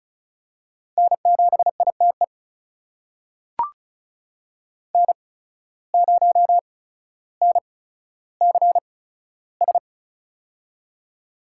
Morse code, heard as a keyed tone, about 36 wpm, spaced as for 15 wpm, 700 Hz. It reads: N7ITE N0NCS
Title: USA Call Signs - 5 Characters - 8x Character Spacing - 35wpm